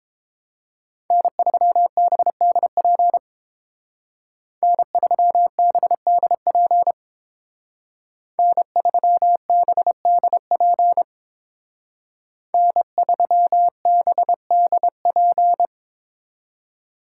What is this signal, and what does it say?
Morse code, heard as a keyed tone, about 30 wpm, 700 Hz. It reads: N3BDP N3BDP N3BDP N3BDP